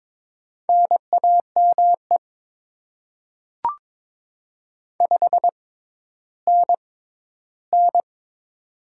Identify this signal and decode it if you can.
Morse code, heard as a keyed tone, about 22 wpm, spaced as for 12 wpm, 690 Hz. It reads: NAME 5NN
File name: Top 20 QSO Elements - 6x Character Spacing - 22wpm